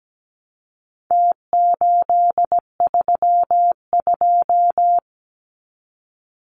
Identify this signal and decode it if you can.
Morse code, heard as a keyed tone, about 17 wpm, 700 Hz. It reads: T832
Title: Numbers - Contest Style - 4-Digit 17wpm